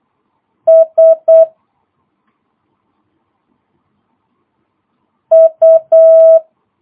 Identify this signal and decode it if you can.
Morse code, heard as a keyed tone, about 8 wpm, 650 Hz. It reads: S U
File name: jh8cbh / 2022cw